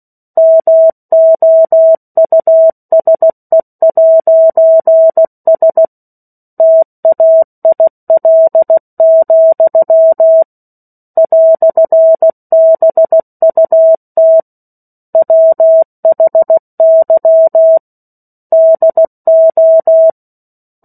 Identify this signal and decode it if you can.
Morse code, heard as a keyed tone, about 16 wpm, 650 Hz. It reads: MOUSE'S TAIL, "BUT WHY DO
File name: 03-Chapter-03 16 WPM